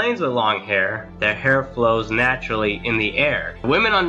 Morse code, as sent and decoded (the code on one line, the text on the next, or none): none